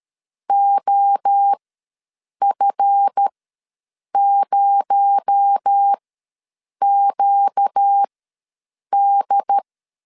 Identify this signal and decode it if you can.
Morse code, heard as a keyed tone, about 13 wpm, spaced as for 7 wpm, 800 Hz. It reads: OF0QD
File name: LEZIONI / 7 GF9Z / 7 GF ws 12wpm 3140